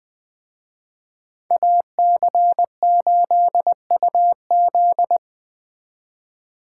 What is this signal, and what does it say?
Morse code, heard as a keyed tone, about 20 wpm, 700 Hz. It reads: AC8UZ